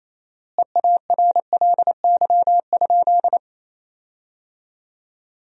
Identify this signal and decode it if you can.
Morse code, heard as a keyed tone, about 28 wpm, 700 Hz. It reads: EARLY?
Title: 4x Word-Spacing - Sentences from Top 400 Words - 28wpm